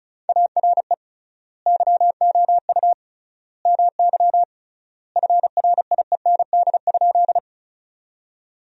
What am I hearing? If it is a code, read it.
Morse code, heard as a keyed tone, about 36 wpm, 700 Hz. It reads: ARE YOU MY FRIEND?
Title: 3x Word-Spacing - Sentences from Top 300 Words - 35wpm